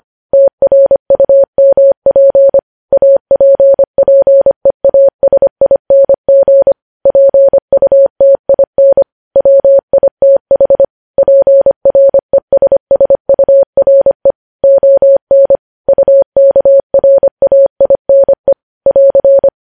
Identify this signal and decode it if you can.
Morse code, heard as a keyed tone, about 25 wpm, 550 Hz. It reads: TRUMP APPEASING PUTIN WITH PRESSURE ON UKRAINE <AR>
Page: news headlines in morse code at 25 wpm